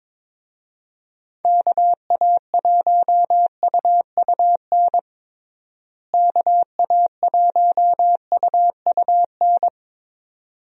Morse code, heard as a keyed tone, about 22 wpm, 700 Hz. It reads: KA1UUN KA1UUN
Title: USA Call Signs - 6 Characters - Repeated 2x - 22wpm